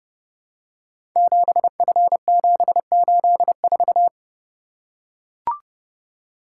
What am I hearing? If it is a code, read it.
Morse code, heard as a keyed tone, about 30 wpm, 700 Hz. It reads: 7F784
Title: Code Groups - 5 Letter:Numbers - 30wpm